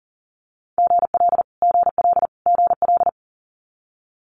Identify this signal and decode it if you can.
Morse code, heard as a keyed tone, about 40 wpm, 700 Hz. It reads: GL GL GL
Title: Multi-letter Phonemes Repeated 3x 40wpm